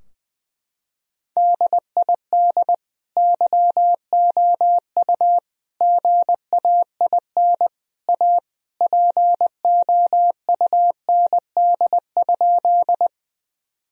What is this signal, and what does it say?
Morse code, heard as a keyed tone, about 20 wpm, 700 Hz. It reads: DID YOU GAIN A POUND?
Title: Sentences from Top 1000 Words - Review Part 2 - 20wpm